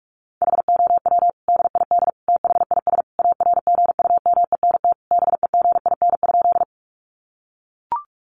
Morse code, heard as a keyed tone, about 44 wpm, 700 Hz. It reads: HOW DID THIS ARGUMENT BEGIN?
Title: Sentences from Top 1400 Words 45wpm